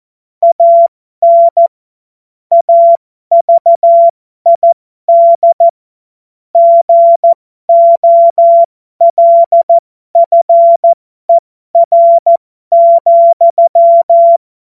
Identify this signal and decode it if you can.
Morse code, heard as a keyed tone, about 14 wpm, spaced as for 12 wpm, 680 Hz. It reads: AN AVID GOLFER,
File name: Morsekurs / Witze